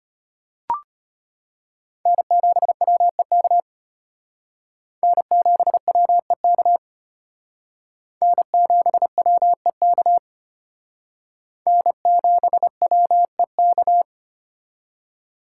Morse code, heard as a keyed tone, about 32 wpm, 700 Hz. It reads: N7WEK N7WEK N7WEK N7WEK